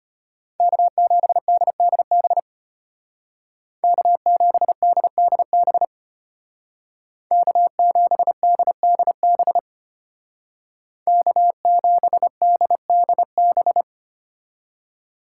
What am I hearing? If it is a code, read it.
Morse code, heard as a keyed tone, about 32 wpm, 700 Hz. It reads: K7DDB K7DDB K7DDB K7DDB